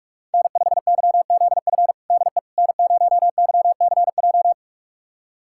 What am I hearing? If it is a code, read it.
Morse code, heard as a keyed tone, about 41 wpm, 700 Hz. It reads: N5YZF DE N0YCJ
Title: USA Callsign DE Callsign - 5 Characters - 45wpm